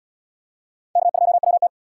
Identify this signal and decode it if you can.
Morse code, heard as a keyed tone, about 77 wpm, 700 Hz. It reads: K2WT